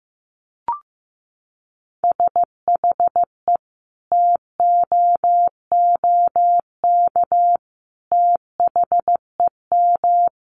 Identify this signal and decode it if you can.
Morse code, heard as a keyed tone, about 15 wpm, 700 Hz. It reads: SHE TOOK THEM
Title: Sentences from Top 1300 Words 15wpm